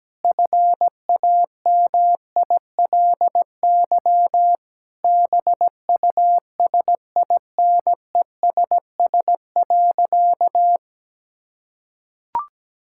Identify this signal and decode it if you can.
Morse code, heard as a keyed tone, about 17 wpm, 700 Hz. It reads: FAMILY BUSINESS.
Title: Sentences from Top 1000 Words - Review Part 5 - 17wpm